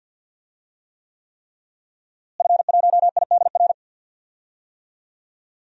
Morse code, heard as a keyed tone, about 51 wpm, 700 Hz. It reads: F1IDR
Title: DXCC-100 Call Signs - 5 Characters - Repeated 3x - 50wpm